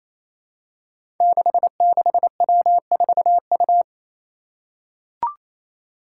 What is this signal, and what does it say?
Morse code, heard as a keyed tone, about 28 wpm, 700 Hz. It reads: 66W4U
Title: Five Challenging Letter-Number Combinations 28wpm